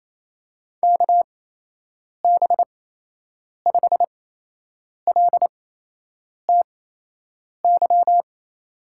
Morse code, heard as a keyed tone, about 28 wpm, spaced as for 8 wpm, 700 Hz. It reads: KB5LTY
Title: USA Call Signs - 6 Characters - 8x Character Spacing - 28wpm